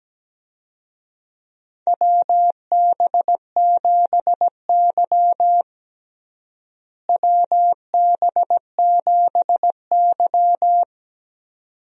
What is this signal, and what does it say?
Morse code, heard as a keyed tone, about 17 wpm, 700 Hz. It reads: WB7Y WB7Y
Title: USA Call Signs - 4 Characters - Repeated 2x - 17wpm